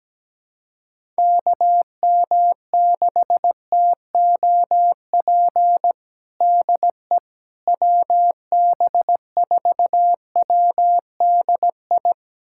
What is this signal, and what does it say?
Morse code, heard as a keyed tone, about 17 wpm, 700 Hz. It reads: KM6TOP DE WB4WDI
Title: USA Callsign DE Callsign - 6 Characters - 17wpm